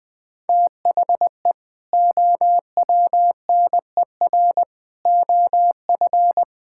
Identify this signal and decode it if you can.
Morse code, heard as a keyed tone, about 19 wpm, 700 Hz. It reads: THE OWNER OF
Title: Sentences with 3-Word Phrases - Spoken Beginning - 20wpm